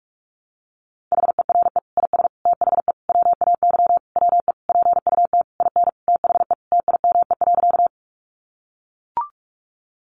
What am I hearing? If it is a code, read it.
Morse code, heard as a keyed tone, about 44 wpm, 700 Hz. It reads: HERE IS THE WAY WE PUT IN THE TIME.
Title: Sentences from Top 200 Words 45wpm